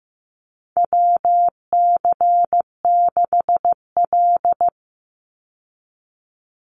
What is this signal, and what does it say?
Morse code, heard as a keyed tone, about 15 wpm, 700 Hz. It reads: WC6L
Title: USA Call Signs 15wpm